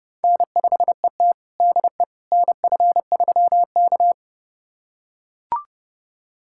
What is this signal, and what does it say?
Morse code, heard as a keyed tone, about 30 wpm, 700 Hz. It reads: N5ET DE NF3K
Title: USA Callsign DE Callsign - 4 Characters - 30wpm